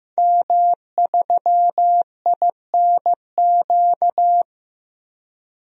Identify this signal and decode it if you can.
Morse code, heard as a keyed tone, about 15 wpm, 700 Hz. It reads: M3INQ
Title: DXCC-100 Call Signs - 5 Characters - 15wpm